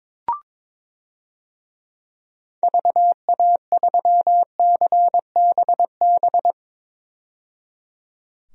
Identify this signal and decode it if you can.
Morse code, heard as a keyed tone, about 21 wpm, 700 Hz. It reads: VA3CBB
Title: DXCC-100 Call Signs - 6 Characters - 22wpm